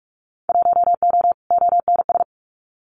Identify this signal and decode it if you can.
Morse code, heard as a keyed tone, about 44 wpm, 700 Hz. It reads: 1O ONS
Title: Winter Field Day Exchanges 45wpm